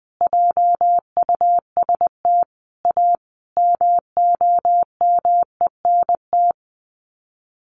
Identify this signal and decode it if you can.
Morse code, heard as a keyed tone, about 20 wpm, 700 Hz. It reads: JUST A MOMENT
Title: Sets of 5 Words Encore 20wpm